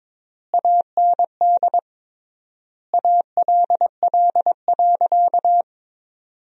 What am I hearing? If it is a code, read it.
Morse code, heard as a keyed tone, about 22 wpm, 700 Hz. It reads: AND ALL.
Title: 3x Word-Spacing - Sentences from Top 200 Words - 22wpm